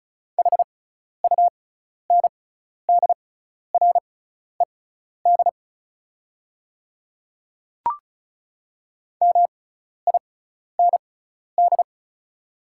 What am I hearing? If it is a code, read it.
Morse code, heard as a keyed tone, about 36 wpm, spaced as for 12 wpm, 700 Hz. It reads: HUNDRED MIND
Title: 6x Character Spacing - Top 500 Words - 35wpm